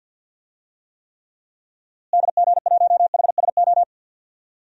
Morse code, heard as a keyed tone, about 53 wpm, 700 Hz. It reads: DG1HSO
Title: DXCC-100 Call Signs - 6 Characters - Repeated 2x - 50wpm